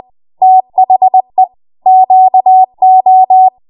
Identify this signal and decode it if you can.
Morse code, heard as a keyed tone, about 20 wpm, 750 Hz. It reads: THE QO
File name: Morse / 20WPM